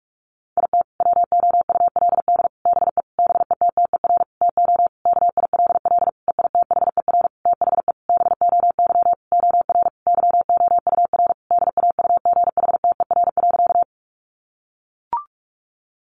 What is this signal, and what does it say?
Morse code, heard as a keyed tone, about 44 wpm, 700 Hz. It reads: IT WOULD BE BETTER TO KILL EITHER THE BOY OR YOUR DAUGHTER.
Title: Sentences from Top 1500 Words 45wpm